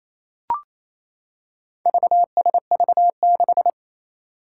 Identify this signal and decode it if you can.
Morse code, heard as a keyed tone, about 28 wpm, 700 Hz. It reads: VSV6
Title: Four Challenging Letter-Number Combinations 28wpm